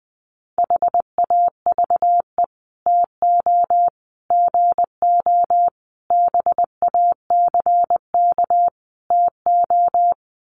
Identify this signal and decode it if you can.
Morse code, heard as a keyed tone, about 20 wpm, 700 Hz. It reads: HAVE TO GO BACK TO
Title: Sets of 5 Words Encore2 20wpm